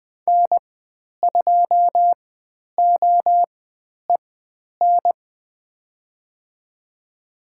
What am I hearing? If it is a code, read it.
Morse code, heard as a keyed tone, about 20 wpm, spaced as for 10 wpm, 700 Hz. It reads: N2OEN
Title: 095 Lesson 16g - Letter c - Call Signs 20wpm